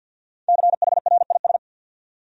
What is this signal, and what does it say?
Morse code, heard as a keyed tone, about 48 wpm, 700 Hz. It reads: CHRIS
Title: USA Common Names - Max 5 Letters - 50wpm